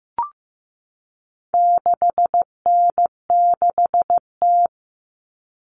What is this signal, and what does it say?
Morse code, heard as a keyed tone, about 15 wpm, 700 Hz. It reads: 6N6T
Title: All Cut Numbers - 4-Digit 15wpm